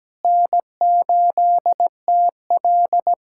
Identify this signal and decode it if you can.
Morse code, heard as a keyed tone, about 17 wpm, 700 Hz. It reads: N8TL
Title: USA Callsign DE Callsign - 4 Characters - 17wpm